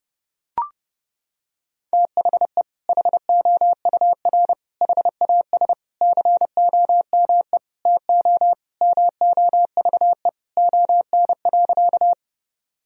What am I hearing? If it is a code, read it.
Morse code, heard as a keyed tone, about 30 wpm, 700 Hz. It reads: THE HOUR HAS COME TO MOVE ON.
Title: Sentences from Top 1000 Words - Review Part 1 - 30wpm